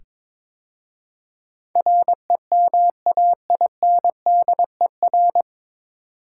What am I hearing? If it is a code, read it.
Morse code, heard as a keyed tone, about 22 wpm, 700 Hz. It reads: REMAINDER
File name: 9 Letter Words - First 3 Letters Given - 22wpm